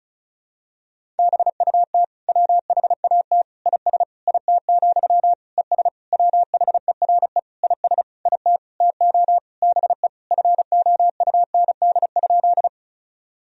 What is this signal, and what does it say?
Morse code, heard as a keyed tone, about 36 wpm, 700 Hz. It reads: BUT WHAT IS IT, ES WHERE IS IT TO BE FOUND?
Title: Sentences from Top 300 Words and Abbreviations 35wpm